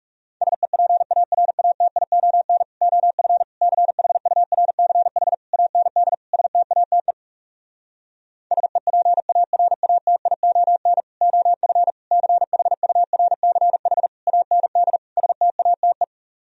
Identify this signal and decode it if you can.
Morse code, heard as a keyed tone, about 40 wpm, 700 Hz. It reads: SEPARATION OF CHURCH AND STATE SEPARATION OF CHURCH AND STATE